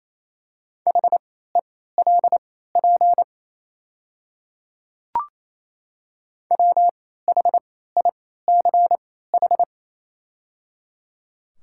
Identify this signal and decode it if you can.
Morse code, heard as a keyed tone, about 27 wpm, spaced as for 16 wpm, 690 Hz. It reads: HELP WHICH
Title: 3x Character Spacing - Top 200 Words - 28wpm